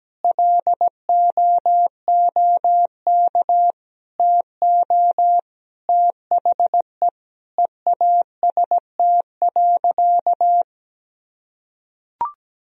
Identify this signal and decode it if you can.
Morse code, heard as a keyed tone, about 17 wpm, 700 Hz. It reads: LOOK TO THE EAST.
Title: Sentences from Top 1000 Words - Review Part 3 - 17wpm